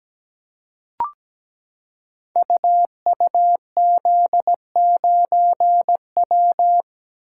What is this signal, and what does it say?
Morse code, heard as a keyed tone, about 17 wpm, 700 Hz. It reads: UUZ9W
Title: Code Groups - 5 Letter:Numbers - 17wpm